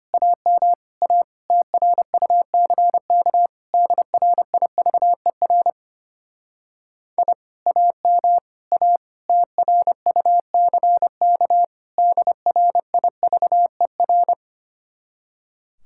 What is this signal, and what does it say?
Morse code, heard as a keyed tone, about 31 wpm, 700 Hz. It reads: AM A TRUCK DRIVER I AM A TRUCK DRIVER
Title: Speed-Racing Common QSO Phrases - 25wpm